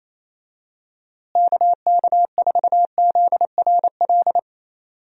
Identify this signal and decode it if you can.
Morse code, heard as a keyed tone, about 27 wpm, 700 Hz. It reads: KK4ZRL
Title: USA Call Signs - 6 Characters - Repeated 2x - 28wpm